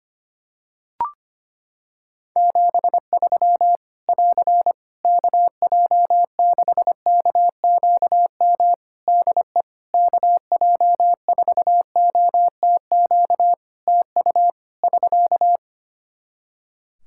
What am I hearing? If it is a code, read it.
Morse code, heard as a keyed tone, about 25 wpm, 700 Hz. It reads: 73 <AR> KJ6KQM DE KJ4OTQ TU <SK>
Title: Practice QSO - Exchange 4 - Ending - 25wpm